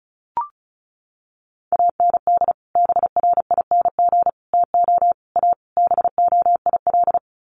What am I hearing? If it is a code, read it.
Morse code, heard as a keyed tone, about 34 wpm, 700 Hz. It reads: AND BRING TO A BOIL
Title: Sets of 5 Words Encore 35wpm